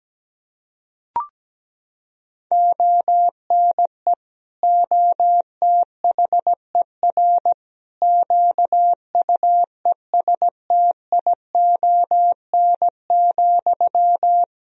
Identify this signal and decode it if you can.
Morse code, heard as a keyed tone, about 17 wpm, 700 Hz. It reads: ONE OTHER QUESTION,
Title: Sentences from Top 400 Words and Abbreviations 17wpm